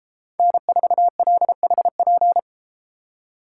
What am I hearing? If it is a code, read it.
Morse code, heard as a keyed tone, about 32 wpm, 700 Hz. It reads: N4LHP